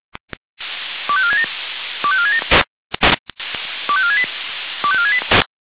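Five-tone selective call (ZVEI), heard as quick runs of five tones, tones of about 70 ms. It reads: 25467 25467 25468 25468
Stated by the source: ZVEI five-tone selective call received on an SDR.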